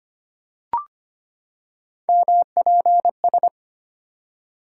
Morse code, heard as a keyed tone, about 25 wpm, 700 Hz. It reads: MPS
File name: Three Letter-Number Combinations - 25wpm